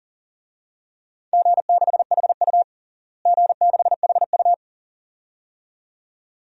Morse code, heard as a keyed tone, about 37 wpm, 700 Hz. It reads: G6HU G6HU
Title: DXCC-100 Call Signs - 4 Characters - Repeated 2x - 40wpm